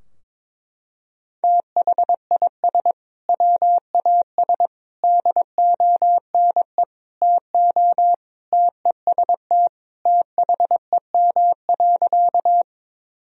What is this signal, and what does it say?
Morse code, heard as a keyed tone, about 21 wpm, 700 Hz. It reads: THIS WAS DONE TO TEST THEM.
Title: Sentences from Top 1000 Words - Review Part 3 - 22wpm